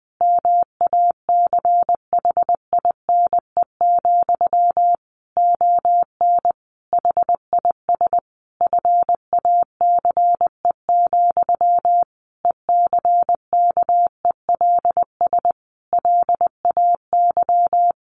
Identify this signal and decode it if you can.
Morse code, heard as a keyed tone, about 20 wpm, 700 Hz. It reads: MACHINE, ON HIS FACE, ECKELS LAY
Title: A Sound Of Thunder 20wpm